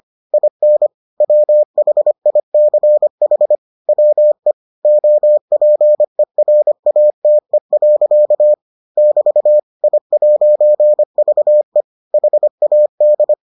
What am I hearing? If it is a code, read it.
Morse code, heard as a keyed tone, about 25 wpm, 600 Hz. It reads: IN WHICH WE OPERATE. <BT> I'VE HAD